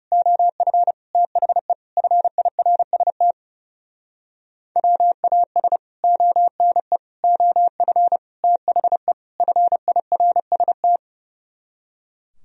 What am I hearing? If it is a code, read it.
Morse code, heard as a keyed tone, about 35 wpm, 700 Hz. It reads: OF THE FIRST WAS ONE OF THE FIRST
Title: Speed-Racing - Sets of 5 Words 30wpm